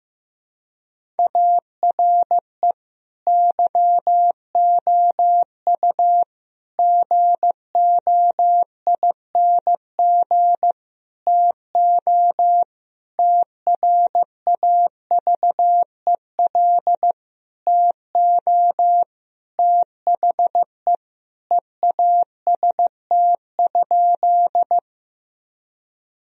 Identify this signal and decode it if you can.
Morse code, heard as a keyed tone, about 15 wpm, 700 Hz. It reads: ARE YOU GOING TO TRAVEL TO THE EAST?
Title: Sentences from Top 1000 Words - Review Part 3 - 15wpm